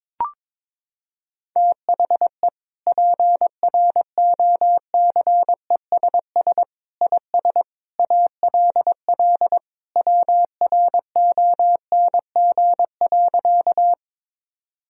Morse code, heard as a keyed tone, about 22 wpm, 700 Hz. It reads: THE PROCESS IS ALL WRONG.
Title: Sentences from Top 1000 Words - Review Part 3 - 22wpm